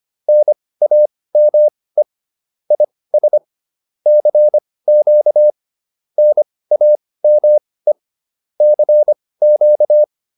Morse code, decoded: NAME IS CQ NAME CQ

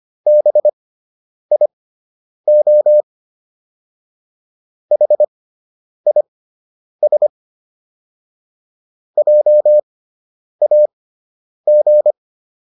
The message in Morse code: BIO HIS JAG

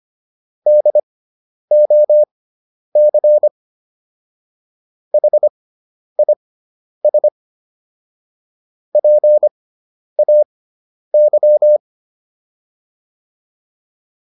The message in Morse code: DOC HIS PAY

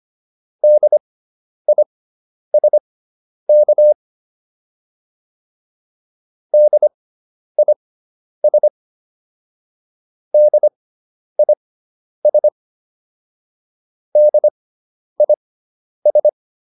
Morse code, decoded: DISK DIS DIS DIS

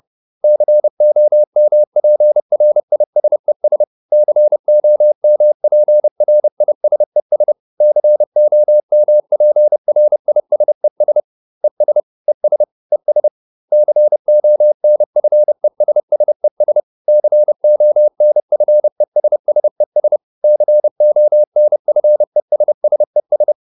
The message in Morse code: COMPRISES COMPRISES COMPRISES ES ES ES CONFESSES CONFESSES CONFESSES

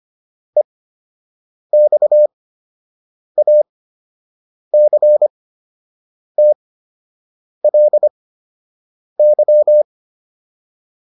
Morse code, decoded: EXACTLY

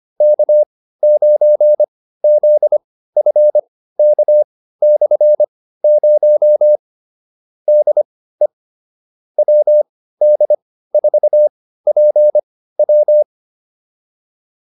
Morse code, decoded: K9ZFK/0 DE WD4PW

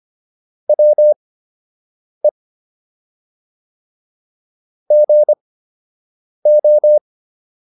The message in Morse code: WE GO

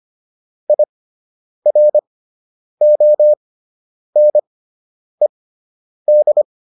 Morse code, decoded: IRONED

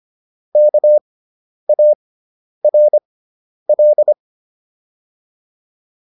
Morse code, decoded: KARL

